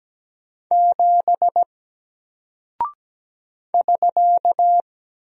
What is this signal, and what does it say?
Morse code, heard as a keyed tone, about 17 wpm, 700 Hz. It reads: 7 <SK>
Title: Single Character (Letter:Number:Punctuation:Prosign) 17wpm